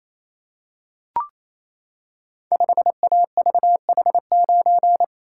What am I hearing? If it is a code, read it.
Morse code, heard as a keyed tone, about 28 wpm, 700 Hz. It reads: 5AVH9
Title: Code Groups - 5 Letter:Numbers - 28wpm